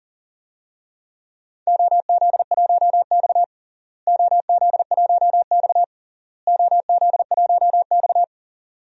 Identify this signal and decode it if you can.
Morse code, heard as a keyed tone, about 40 wpm, 700 Hz. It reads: OZ1X OZ1X OZ1X
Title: DXCC-100 Call Signs - 4 Characters - Repeated 3x - 40wpm